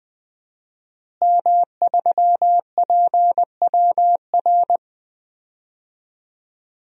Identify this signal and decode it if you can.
Morse code, heard as a keyed tone, about 20 wpm, 700 Hz. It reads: M3PWR
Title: DXCC-100 Call Signs - 5 Characters - Repeated 2x - 20wpm